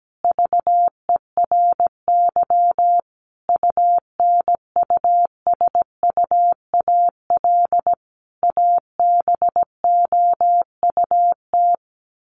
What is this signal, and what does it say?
Morse code, heard as a keyed tone, about 17 wpm, 700 Hz. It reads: VERY UNUSUAL ABOUT
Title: Sentences from Top 1400 Words 17wpm